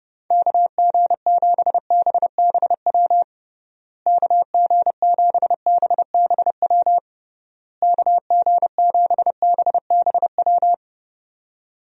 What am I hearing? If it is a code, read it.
Morse code, heard as a keyed tone, about 30 wpm, 700 Hz. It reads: KG7BBW KG7BBW KG7BBW
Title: USA Call Signs - 6 Characters - Repeated 3x - 30wpm